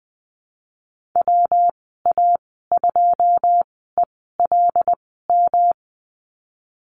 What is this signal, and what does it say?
Morse code, heard as a keyed tone, about 20 wpm, spaced as for 15 wpm, 700 Hz. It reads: WA2ELM
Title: 128 Lesson 21g - Number 6 - Call Signs 20wpm